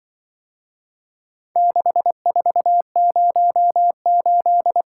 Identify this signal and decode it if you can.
Morse code, heard as a keyed tone, about 24 wpm, 700 Hz. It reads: 6408
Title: Speed-Racing - Four Letter-Number Combinations - 20wpm